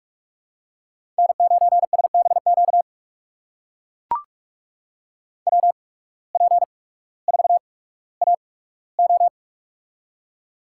Morse code, heard as a keyed tone, about 44 wpm, spaced as for 19 wpm, 700 Hz. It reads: N9SBQ WP4AO
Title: USA Call Signs - 5 Characters - 8x Character Spacing - 45wpm